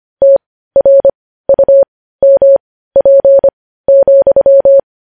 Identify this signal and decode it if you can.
Morse code, heard as a keyed tone, about 25 wpm, spaced as for 15 wpm, 550 Hz. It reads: TRUMP,
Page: news-headlines-in-morse-code-at-15-wpm